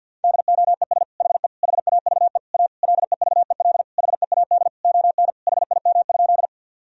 Morse code, heard as a keyed tone, about 50 wpm, 700 Hz. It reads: DOES HE HAVE A LEVEL HEAD ON HIM?
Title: Sentences from Top 1000 Words - Review Part 4 - 50wpm